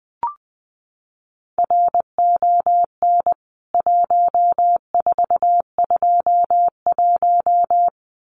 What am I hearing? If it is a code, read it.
Morse code, heard as a keyed tone, about 20 wpm, 700 Hz. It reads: RON 1421
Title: CWT Exchange 20wpm